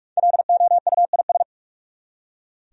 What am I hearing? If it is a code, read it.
Morse code, heard as a keyed tone, about 46 wpm, 700 Hz. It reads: LOUIS
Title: USA Common Names - 45wpm